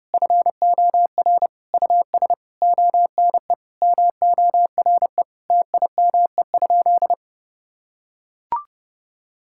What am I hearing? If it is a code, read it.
Morse code, heard as a keyed tone, about 30 wpm, 700 Hz. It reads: FOR US ONE MORE TIME?
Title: Sentences from Top 1000 Words - Review Part 5 - 30wpm